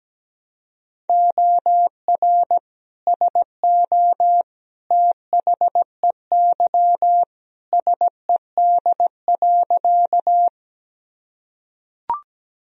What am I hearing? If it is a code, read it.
Morse code, heard as a keyed tone, about 17 wpm, 700 Hz. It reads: OR SO THEY SED.
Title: Sentences from Top 100 Words and Abbreviations 17wpm